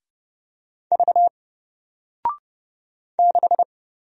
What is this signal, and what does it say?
Morse code, heard as a keyed tone, about 30 wpm, 700 Hz. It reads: V 6
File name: Single-Letter-Number-30wpm